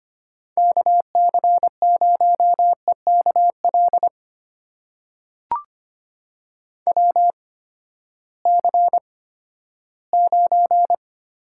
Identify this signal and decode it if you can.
Morse code, heard as a keyed tone, about 25 wpm, spaced as for 14 wpm, 700 Hz. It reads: KC0EKL WC9
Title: USA Call Signs - 8x Character Spacing - 25wpm